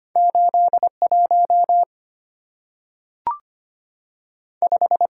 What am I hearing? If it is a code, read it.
Morse code, heard as a keyed tone, about 25 wpm, 690 Hz. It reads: 81 5
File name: Two Letter-Number Combinations - 25wpm